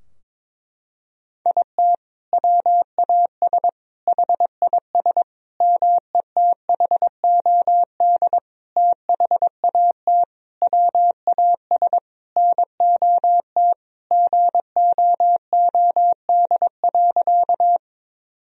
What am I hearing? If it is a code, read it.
Morse code, heard as a keyed tone, about 22 wpm, 700 Hz. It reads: IT WAS HIS METHOD THAT WAS NOT GOOD.
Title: Sentences from Top 1000 Words - Review Part 3 - 22wpm